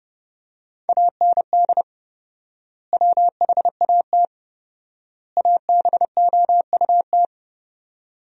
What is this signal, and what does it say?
Morse code, heard as a keyed tone, about 30 wpm, 700 Hz. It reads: AND WHAT ABOUT
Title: 4x Word-Spacing - Sentences from Top 200 Words - 30wpm